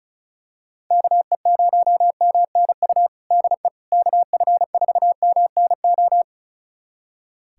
Morse code, heard as a keyed tone, about 34 wpm, 700 Hz. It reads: KE0MNU DE KF4MNO
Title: USA Callsign DE Callsign - 6 Characters - 35wpm